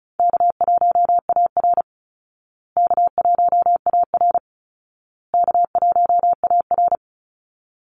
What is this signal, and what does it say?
Morse code, heard as a keyed tone, about 34 wpm, 700 Hz. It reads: K1AR K1AR K1AR
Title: Easy Call Signs Repeated 3x - 35wpm